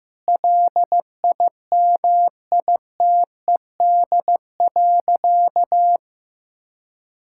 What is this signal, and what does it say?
Morse code, heard as a keyed tone, about 16 wpm, 700 Hz. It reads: LIMITED.
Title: Sentences from Top 1000 Words - Review Part 6 - 15wpm